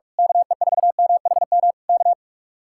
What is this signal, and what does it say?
Morse code, heard as a keyed tone, about 41 wpm, 700 Hz. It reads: KE4MHM K
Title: Speed-Racing Common QSO Phrases - 45wpm